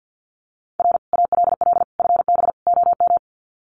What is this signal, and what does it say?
Morse code, heard as a keyed tone, about 50 wpm, 700 Hz. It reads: R ALL FB OM
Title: Common QSO Phrases 50wpm